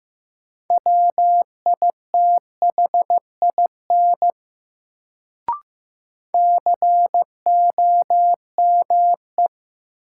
Morse code, heard as a keyed tone, about 15 wpm, 700 Hz. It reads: WITHIN COME
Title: Sets of 2 Words - New Word Drill - 15wpm